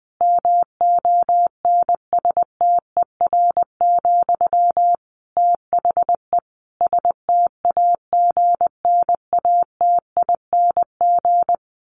Morse code, heard as a keyed tone, about 20 wpm, 700 Hz. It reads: MONSTER, THE STAGNATING
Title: A Sound Of Thunder 20wpm